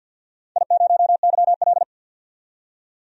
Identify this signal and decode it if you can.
Morse code, heard as a keyed tone, about 51 wpm, 700 Hz. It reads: I0YL
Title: DXCC-100 Call Signs - 4 Characters - 50wpm